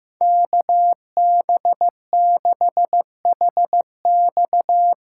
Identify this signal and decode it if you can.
Morse code, heard as a keyed tone, about 15 wpm, 700 Hz. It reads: KB6HX
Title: USA Call Signs - 6 Characters - 6x Character Spacing - 15wpm